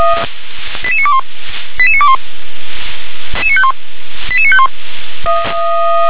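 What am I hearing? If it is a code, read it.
Five-tone selective call (ZVEI), heard as quick runs of five tones, tones of about 70 ms. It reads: 88921 88921 88951
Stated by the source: ZVEI five-tone selective call received by SDR.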